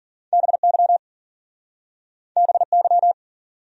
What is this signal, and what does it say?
Morse code, heard as a keyed tone, about 51 wpm, 700 Hz. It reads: BY BY